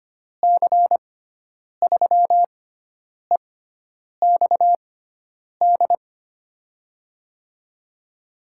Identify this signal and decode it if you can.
Morse code, heard as a keyed tone, about 25 wpm, spaced as for 9 wpm, 700 Hz. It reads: C3EXD
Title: USA Call Signs - 6 Characters - 6x Character Spacing - 25wpm